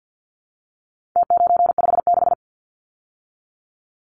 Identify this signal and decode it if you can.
Morse code, heard as a keyed tone, about 50 wpm, 700 Hz. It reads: T956
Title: Numbers - Contest Style - 4-Digit 50wpm